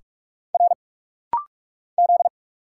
Morse code, heard as a keyed tone, about 44 wpm, 700 Hz. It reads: R Z
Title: Single Letter-Number - Rapid-Fire 45wpm